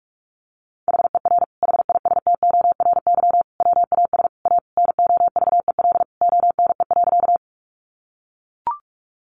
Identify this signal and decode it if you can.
Morse code, heard as a keyed tone, about 44 wpm, 700 Hz. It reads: HER HISTORY WAS A NOVEL ONE.
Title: Sentences from Top 1300 Words 45wpm